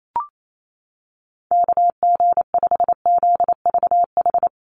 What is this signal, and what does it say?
Morse code, heard as a keyed tone, about 28 wpm, 700 Hz. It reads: KG5ZVH
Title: USA Call Signs Repeated 2x 28wpm